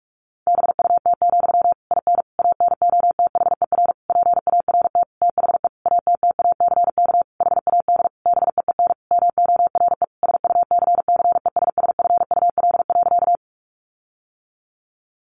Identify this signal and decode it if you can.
Morse code, heard as a keyed tone, about 44 wpm, 700 Hz. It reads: BUT, IN ANOTHER PART THE ATTACK HAD BEEN MORE SUCCESSFUL.
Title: Sentences from Top 1500 Words 45wpm